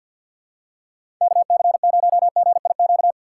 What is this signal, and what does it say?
Morse code, heard as a keyed tone, about 47 wpm, 700 Hz. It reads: KC0GIQ
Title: USA Call Signs - 6 Characters - 3x Character Spacing - 50wpm